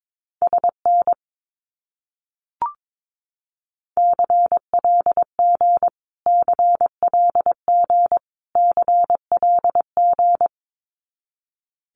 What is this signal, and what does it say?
Morse code, heard as a keyed tone, about 22 wpm, 700 Hz. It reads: SN CLG CLG CLG
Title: Abbreviations Repeated 3x - 22wpm